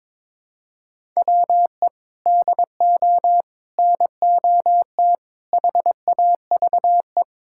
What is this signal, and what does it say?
Morse code, heard as a keyed tone, about 22 wpm, 700 Hz. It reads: WE DO NOT HAVE
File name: practice / 22wpm / Sentences with 3-Word Phrases - 22wpm